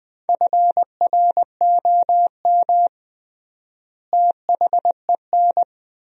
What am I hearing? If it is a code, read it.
Morse code, heard as a keyed tone, about 20 wpm, 700 Hz. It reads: FROM THEN
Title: 3x Word-Spacing - Sentences from Top 400 Words - 20wpm